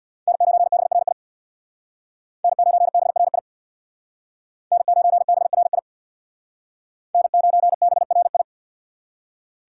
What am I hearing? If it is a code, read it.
Morse code, heard as a keyed tone, about 62 wpm, 700 Hz. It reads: N9BRI N9BRI N9BRI N9BRI
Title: Speed Racing - USA Call Signs 50wpm